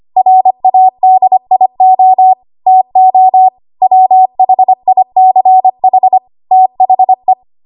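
Morse code, heard as a keyed tone, about 25 wpm, 750 Hz. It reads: RADIO TO WHICH THE
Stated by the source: ARRL 25 WPM code practice archive.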